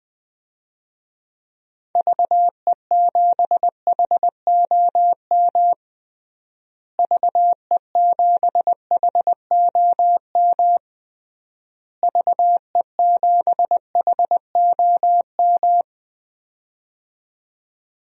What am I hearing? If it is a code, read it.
Morse code, heard as a keyed tone, about 21 wpm, 700 Hz. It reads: VE7HOM VE7HOM VE7HOM
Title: DXCC-100 Call Signs - 6 Characters - Repeated 3x - 20wpm